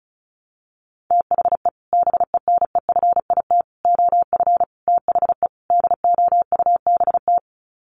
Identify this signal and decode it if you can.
Morse code, heard as a keyed tone, about 34 wpm, 700 Hz. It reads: THE BENEFIT OF THE DOUBT
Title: Sets of 5 Words Encore 35wpm